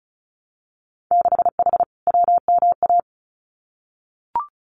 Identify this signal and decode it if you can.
Morse code, heard as a keyed tone, about 34 wpm, 700 Hz. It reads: 6H WMA
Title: Winter Field Day Exchanges Encore 35wpm